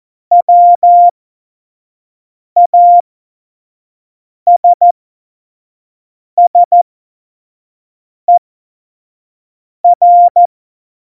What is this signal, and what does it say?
Morse code, heard as a keyed tone, about 14 wpm, spaced as for 5 wpm, 700 Hz. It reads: WASSER